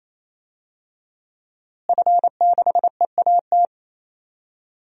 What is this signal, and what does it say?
Morse code, heard as a keyed tone, about 28 wpm, 700 Hz. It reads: F6EAT
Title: DXCC-100 Call Signs - 5 Characters - Repeated 3x - 28wpm